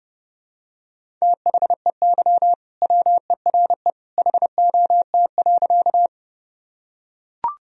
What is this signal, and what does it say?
Morse code, heard as a keyed tone, about 32 wpm, 700 Hz. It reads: THEY WERE HOT.
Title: Sentences from Top 1000 Words - Review Part 3 - 30wpm